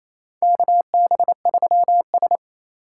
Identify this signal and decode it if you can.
Morse code, heard as a keyed tone, about 28 wpm, 700 Hz. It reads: KB3S
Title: USA Call Signs - 4 Characters - 4x Character Spacing - 28wpm